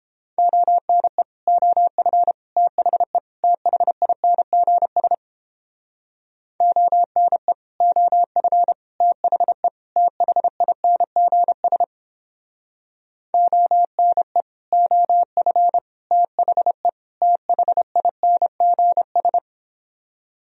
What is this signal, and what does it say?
Morse code, heard as a keyed tone, about 29 wpm, 700 Hz. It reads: ONE OF THE THINGS ONE OF THE THINGS ONE OF THE THINGS